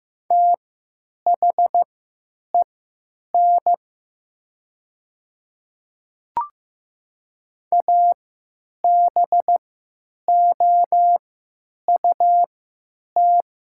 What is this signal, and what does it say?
Morse code, heard as a keyed tone, about 15 wpm, spaced as for 9 wpm, 700 Hz. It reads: THEN ABOUT
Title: 3x Character Spacing - Top 200 Words - 15wpm